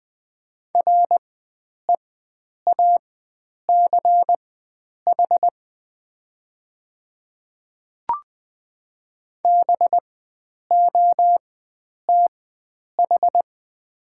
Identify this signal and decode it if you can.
Morse code, heard as a keyed tone, about 20 wpm, spaced as for 9 wpm, 700 Hz. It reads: REACH BOTH
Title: 4x Character Spacing - Top 500 Words - 20wpm